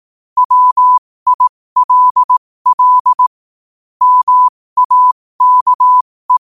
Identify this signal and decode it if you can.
Morse code, heard as a keyed tone, about 18 wpm, spaced as for 15 wpm, 1000 Hz. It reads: WILL MAKE
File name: qotdfiles / 15wpm